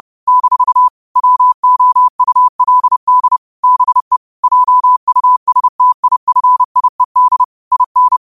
Morse code, heard as a keyed tone, about 30 wpm, 1000 Hz. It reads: - WOULD BE JUSTIFIED IN
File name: qotdfiles / 30wpm